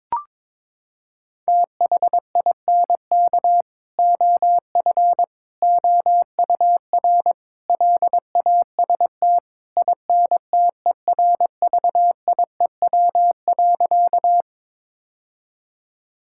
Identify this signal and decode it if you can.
Morse code, heard as a keyed tone, about 22 wpm, 700 Hz. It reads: THINK OF OUR LAST INTERVIEW.
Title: Sentences from Top 1600 Words 22wpm